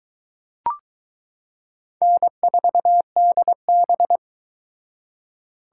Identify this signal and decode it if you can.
Morse code, heard as a keyed tone, about 22 wpm, 700 Hz. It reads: N4DB